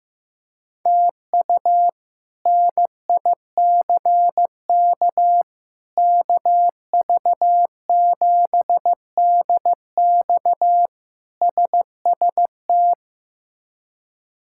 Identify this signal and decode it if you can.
Morse code, heard as a keyed tone, about 15 wpm, 700 Hz. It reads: TU NICK KV7DX SST
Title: K1USN SST QSO Practice - 15wpm